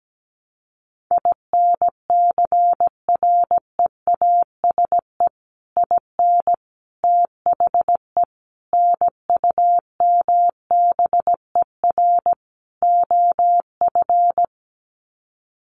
Morse code, heard as a keyed tone, about 17 wpm, 700 Hz. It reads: INCREASE IN THE NUMBER OF
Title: Sets of 5 Words 17wpm